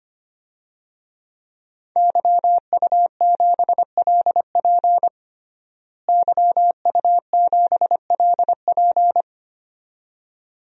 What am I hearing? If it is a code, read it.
Morse code, heard as a keyed tone, about 25 wpm, 700 Hz. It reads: YU7LP YU7LP
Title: DXCC-100 Call Signs - 5 Characters - Repeated 2x - 25wpm